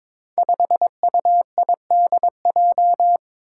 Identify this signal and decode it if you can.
Morse code, heard as a keyed tone, about 22 wpm, 700 Hz. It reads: 5UIDJ
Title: Five Challenging Letter-Number Combinations 22wpm